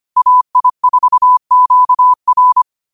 Morse code, heard as a keyed tone, about 25 wpm, 1000 Hz. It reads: AI4QR
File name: 25wpm